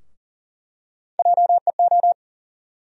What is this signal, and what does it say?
Morse code, heard as a keyed tone, about 40 wpm, 700 Hz. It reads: JEO
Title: USA Call Sign Postfixes - 3 Letters 40wpm